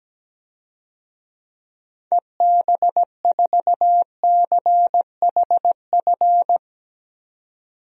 Morse code, heard as a keyed tone, about 17 wpm, 700 Hz. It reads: EB4CHF
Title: DXCC-100 Call Signs - 6 Characters - 17wpm